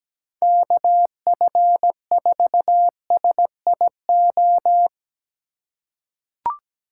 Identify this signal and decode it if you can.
Morse code, heard as a keyed tone, about 17 wpm, 700 Hz. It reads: KF4SIO
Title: USA Call Signs - 3x Character Spacing - 17wpm